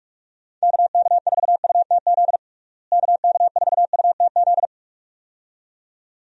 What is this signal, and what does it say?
Morse code, heard as a keyed tone, about 46 wpm, 700 Hz. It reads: KK4UTZ KK4UTZ
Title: USA Call Signs - 6 Characters - Repeated 2x - 45wpm